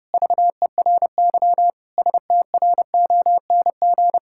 Morse code, heard as a keyed tone, about 30 wpm, 700 Hz. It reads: VERY STRONG
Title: Sentences from Top 1000 Words - Review Part 5 - 30wpm